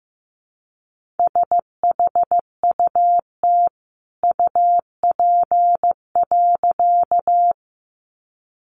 Morse code, heard as a keyed tone, about 15 wpm, 700 Hz. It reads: SHUT UP.
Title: A Sound Of Thunder 15wpm